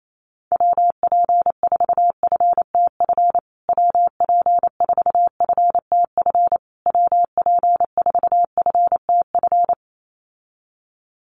Morse code, heard as a keyed tone, about 28 wpm, 700 Hz. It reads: WP4FTF WP4FTF WP4FTF
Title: USA Call Signs Repeated 3x 28wpm